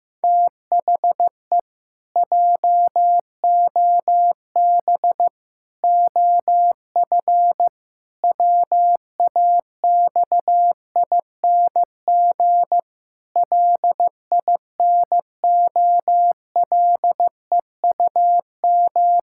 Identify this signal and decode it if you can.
Morse code, heard as a keyed tone, about 15 wpm, 700 Hz. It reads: THE JOB OF WAXING LINOLEUM
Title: Pangram Challenge v2 - 15wpm